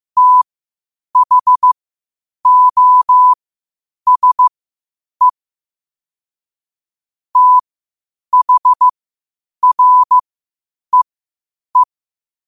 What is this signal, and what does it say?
Morse code, heard as a keyed tone, about 15 wpm, spaced as for 8 wpm, 1000 Hz. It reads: THOSE THREE
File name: qotdfiles / 8wpm